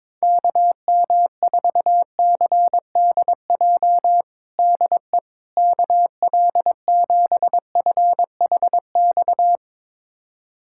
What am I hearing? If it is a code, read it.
Morse code, heard as a keyed tone, about 22 wpm, 700 Hz. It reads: KM4CDJ DE KL7FHX
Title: USA Callsign DE Callsign - 6 Characters - 22wpm